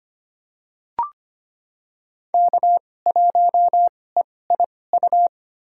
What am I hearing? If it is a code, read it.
Morse code, heard as a keyed tone, about 25 wpm, spaced as for 18 wpm, 700 Hz. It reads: K1EIU
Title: USA Call Signs - 2x Character Spacing - 25wpm